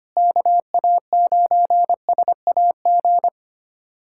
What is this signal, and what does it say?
Morse code, heard as a keyed tone, about 25 wpm, 700 Hz. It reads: KA9SAG
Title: USA Callsign DE Callsign - 6 Characters - 25wpm